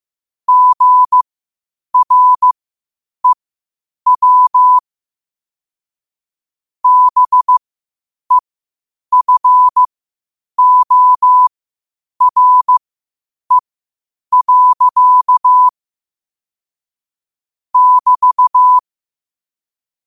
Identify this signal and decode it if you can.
Morse code, heard as a keyed tone, about 15 wpm, spaced as for 8 wpm, 1000 Hz. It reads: GREW BEFORE. <BT>
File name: qotdfiles / 8wpm